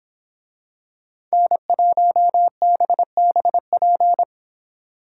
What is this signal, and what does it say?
Morse code, heard as a keyed tone, about 25 wpm, 700 Hz. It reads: N1BBP
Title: Speed Racing - USA 1x3 Call Signs 22wpm